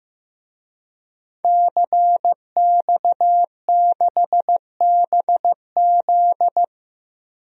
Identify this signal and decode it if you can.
Morse code, heard as a keyed tone, about 15 wpm, 700 Hz. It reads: CX6BZ
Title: DXCC-100 Call Signs - 5 Characters - Repeated 2x - 15wpm